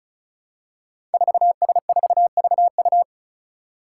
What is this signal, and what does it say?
Morse code, heard as a keyed tone, about 36 wpm, 700 Hz. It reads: 4S4VU
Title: Five Challenging Letter-Number Combinations 35wpm